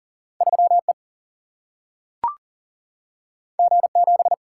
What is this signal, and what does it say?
Morse code, heard as a keyed tone, about 40 wpm, 700 Hz. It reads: 3 E G 7
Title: Single-Letter Single-Number 40wpm